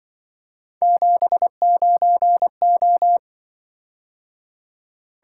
Morse code, heard as a keyed tone, about 25 wpm, 700 Hz. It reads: 79O